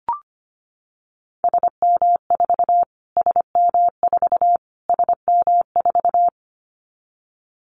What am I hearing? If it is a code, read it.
Morse code, heard as a keyed tone, about 25 wpm, 700 Hz. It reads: SM4 SM4 SM4
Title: Callsign Prefixes for Top 25 DXCC - Repeated 3x - 25wpm